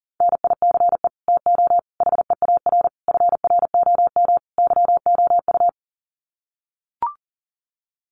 Morse code, heard as a keyed tone, about 40 wpm, 700 Hz. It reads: NICE TO HEAR FROM YOU
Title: Common QSO Phrases 40wpm